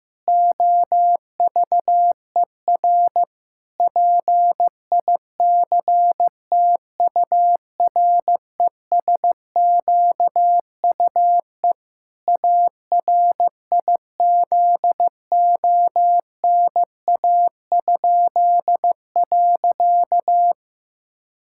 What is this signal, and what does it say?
Morse code, heard as a keyed tone, about 16 wpm, 700 Hz. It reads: OVER PICTURESQUE ARIZONA?.
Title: Pangram Challenge v2 - 15wpm